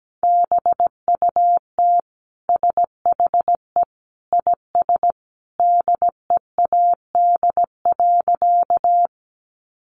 Morse code, heard as a keyed tone, about 17 wpm, 700 Hz. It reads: BUT SHE IS DEAD.
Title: Sentences from Top 1500 Words 17wpm